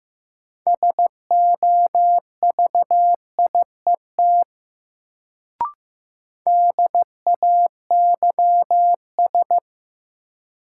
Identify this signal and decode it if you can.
Morse code, heard as a keyed tone, about 15 wpm, 700 Hz. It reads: SOVIET DAYS
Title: Sets of 5 Words Encore - New Word Drill - 15wpm